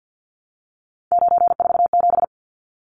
Morse code, heard as a keyed tone, about 50 wpm, 700 Hz. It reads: 947
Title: Numbers - 3-Digit 50wpm